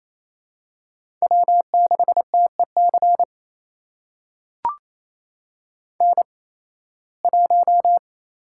Morse code, heard as a keyed tone, about 28 wpm, spaced as for 18 wpm, 700 Hz. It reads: W6TEC N1
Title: USA Call Signs - 5 Characters - 8x Character Spacing - 28wpm